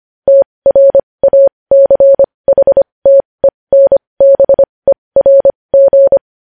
Morse code, heard as a keyed tone, about 25 wpm, spaced as for 20 wpm, 550 Hz. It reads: TRACHTENBERG
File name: podcast / News headlines-1x-25x20